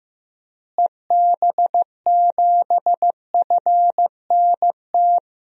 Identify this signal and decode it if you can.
Morse code, heard as a keyed tone, about 15 wpm, 700 Hz. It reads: EB7FNT
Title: DXCC-100 Call Signs - 6 Characters - Repeated 2x - 15wpm